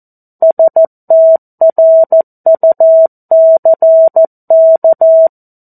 Morse code, heard as a keyed tone, about 14 wpm, 650 Hz. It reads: STRUCK
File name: 02-Chapter-02 14 WPM